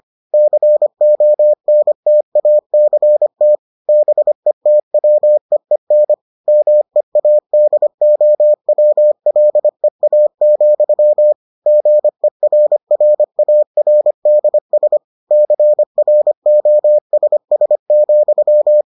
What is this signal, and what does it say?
Morse code, heard as a keyed tone, about 25 wpm, 600 Hz. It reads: CONTACT BETWEEN MEADOWLEA, GERRARDS CROSS,